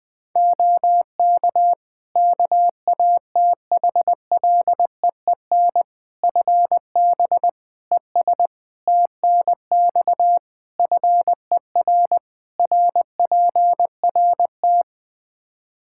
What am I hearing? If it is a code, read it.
Morse code, heard as a keyed tone, about 20 wpm, 700 Hz. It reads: OK KATHLEEN FB ES TNX FER RPRT
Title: Practice QSO - Exchange 2 - Info - 20wpm